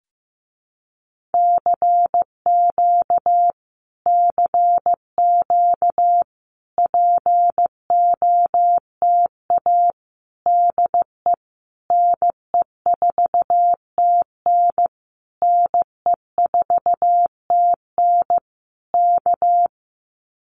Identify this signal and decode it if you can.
Morse code, heard as a keyed tone, about 15 wpm, 700 Hz. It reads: CQ CQ POTA DE NE4TN NE4TN K
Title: Common QSO Phrases Encore 15wpm